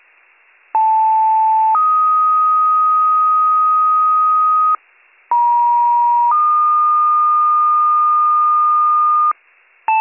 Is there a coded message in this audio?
Yes: two-tone page from 0.7 to 9.3 s, heard as a one-second tone, then a three-second tone, about 880 then 1250 Hz, about 950 then 1220 Hz.